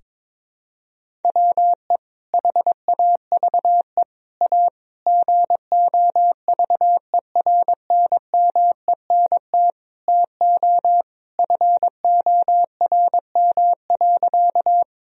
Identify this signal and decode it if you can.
Morse code, heard as a keyed tone, about 22 wpm, 700 Hz. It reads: WE HAVE A GOVERNMENT TO FORM.
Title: Sentences from Top 1000 Words - Review Part 1 - 22wpm